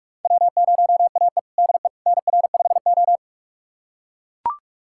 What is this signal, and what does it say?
Morse code, heard as a keyed tone, about 41 wpm, 700 Hz. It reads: W0AE DE NR5O
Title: USA Callsign DE Callsign - 4 Characters - 45wpm